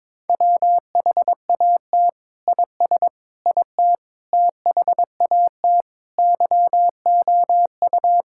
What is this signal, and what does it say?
Morse code, heard as a keyed tone, about 22 wpm, 700 Hz. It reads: WHAT IS IT THAT YOU
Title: Sentences from Top 1600 Words 22wpm